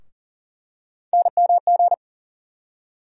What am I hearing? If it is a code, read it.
Morse code, heard as a keyed tone, about 37 wpm, 700 Hz. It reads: NMG
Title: USA Call Sign Postfixes - 3 Letters 40wpm